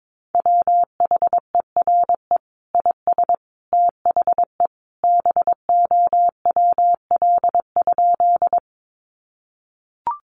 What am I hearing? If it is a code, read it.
Morse code, heard as a keyed tone, about 22 wpm, 700 Hz. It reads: WHERE IS THE BOWL?
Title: Sentences from Top 1200 Words 22wpm